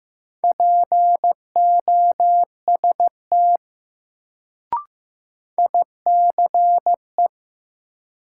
Morse code, heard as a keyed tone, about 15 wpm, 700 Hz. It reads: POST ICE
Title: Sentences from Top 1000 Words - New Word Drill - 15wpm